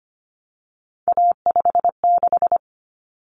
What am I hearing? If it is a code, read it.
Morse code, heard as a keyed tone, about 25 wpm, 700 Hz. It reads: A56
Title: Common Cut Numbers - 3-Digit 25wpm